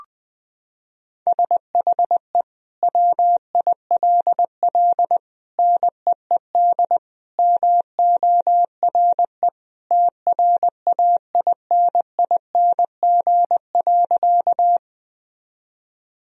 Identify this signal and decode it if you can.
Morse code, heard as a keyed tone, about 20 wpm, 700 Hz. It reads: SHE WILL NEED MORE TRAINING.
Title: Sentences from Top 1000 Words - Review Part 4 - 20wpm